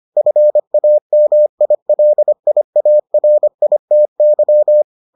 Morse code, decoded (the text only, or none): FAMILIARITY